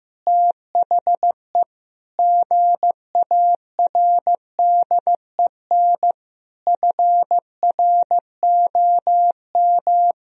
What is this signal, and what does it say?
Morse code, heard as a keyed tone, about 15 wpm, 700 Hz. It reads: THE GARDEN FROM
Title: Sentences from Top 1000 Words - Review Part 4 - 15wpm